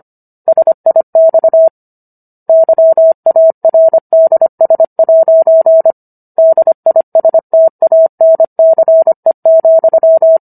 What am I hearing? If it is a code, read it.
Morse code, heard as a keyed tone, about 25 wpm, 650 Hz. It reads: SIX YARDS' DISTANCE,